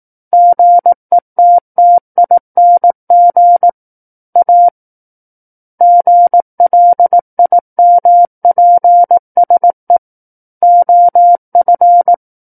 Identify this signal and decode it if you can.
Morse code, heard as a keyed tone, about 18 wpm, 700 Hz. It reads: GETTING A GLIMPSE OF
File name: 18 WPM